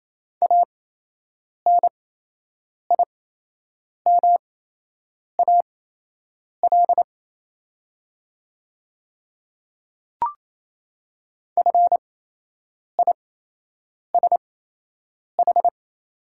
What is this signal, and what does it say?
Morse code, heard as a keyed tone, about 28 wpm, spaced as for 8 wpm, 700 Hz. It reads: ANIMAL FISH
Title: 8x Character Spacing - Top 300 Words - 28wpm